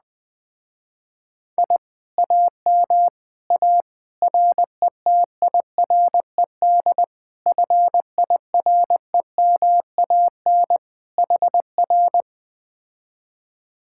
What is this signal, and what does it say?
Morse code, heard as a keyed tone, about 20 wpm, 700 Hz. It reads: I AM A RETIRED FIREMAN HR